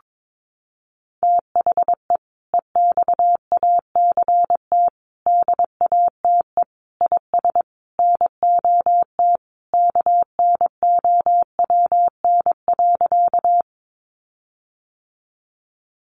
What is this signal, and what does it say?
Morse code, heard as a keyed tone, about 22 wpm, 700 Hz. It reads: THE EXACT DATE IS NOT KNOWN.
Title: Sentences from Top 1300 Words 22wpm